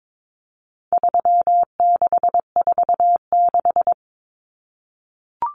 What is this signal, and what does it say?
Morse code, heard as a keyed tone, about 22 wpm, 700 Hz. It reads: 3646
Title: Common Cut Numbers - 4-Digit 22wpm